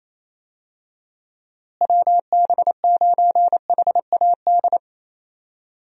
Morse code, heard as a keyed tone, about 27 wpm, 700 Hz. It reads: WB9HAD
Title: USA Call Signs - 6 Characters - Repeated 3x - 28wpm